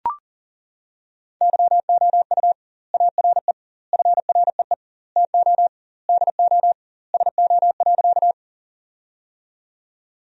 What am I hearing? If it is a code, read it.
Morse code, heard as a keyed tone, about 40 wpm, 700 Hz. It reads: YOU ARE FREE TO DO SO.
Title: 2x Word-Spacing - Sentences from Top 500 Words - 40wpm